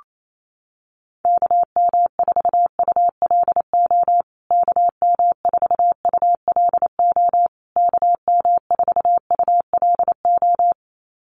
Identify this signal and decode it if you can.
Morse code, heard as a keyed tone, about 28 wpm, 700 Hz. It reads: KM4ULO KM4ULO KM4ULO
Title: USA Call Signs Repeated 3x 28wpm